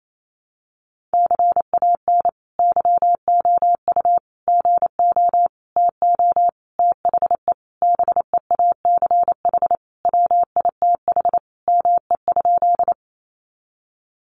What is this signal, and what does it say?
Morse code, heard as a keyed tone, about 28 wpm, 700 Hz. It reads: CAN YOU GO TO THE BEACH WITH ME?
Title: Sentences from Top 1500 Words 28wpm